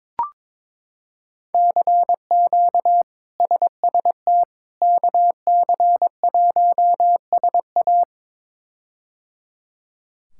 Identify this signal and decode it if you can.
Morse code, heard as a keyed tone, about 22 wpm, 700 Hz. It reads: CQ SST KC1SA
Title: K1USN SST QSO Practice - 22wpm